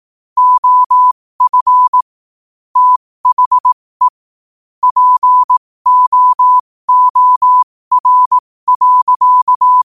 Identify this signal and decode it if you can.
Morse code, heard as a keyed tone, about 18 wpm, spaced as for 15 wpm, 1000 Hz. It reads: OF THE POOR.
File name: qotdfiles / 15wpm